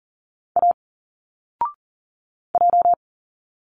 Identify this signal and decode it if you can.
Morse code, heard as a keyed tone, about 40 wpm, 700 Hz. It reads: A J
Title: Single Letters 40wpm